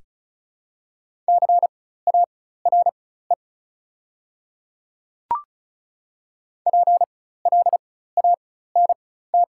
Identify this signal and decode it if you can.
Morse code, heard as a keyed tone, about 36 wpm, spaced as for 16 wpm, 700 Hz. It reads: CARE PLANT
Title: 4x Character Spacing - Top 300 Words - 35wpm